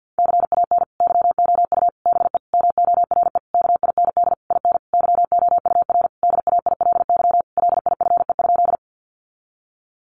Morse code, heard as a keyed tone, about 50 wpm, 700 Hz. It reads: CAN YOU BE MORE KIND IN YOUR DAILY LIFE?
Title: Sentences from Top 1100 Words 50wpm